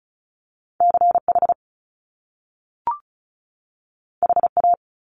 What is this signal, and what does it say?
Morse code, heard as a keyed tone, about 36 wpm, 700 Hz. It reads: CH HA
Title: Top Two Letter Combinations in Words 35wpm